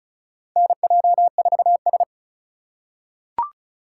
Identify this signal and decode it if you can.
Morse code, heard as a keyed tone, about 36 wpm, 700 Hz. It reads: NJ4S
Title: USA Call Signs - 2x Character Spacing - 35wpm